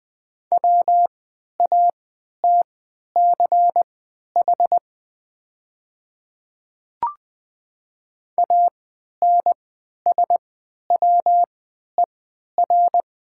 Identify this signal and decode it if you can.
Morse code, heard as a keyed tone, about 21 wpm, spaced as for 11 wpm, 700 Hz. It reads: WATCH ANSWER
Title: 3x Character Spacing - Top 300 Words - 20wpm